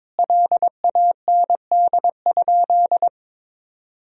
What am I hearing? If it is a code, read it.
Morse code, heard as a keyed tone, about 22 wpm, 700 Hz. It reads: LAND?
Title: Sentences from Top 1000 Words - Review Part 1 - 22wpm